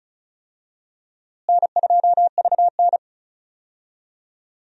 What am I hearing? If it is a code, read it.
Morse code, heard as a keyed tone, about 34 wpm, 700 Hz. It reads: N2VN
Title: USA Call Signs - 4 Characters - Repeated 2x - 35wpm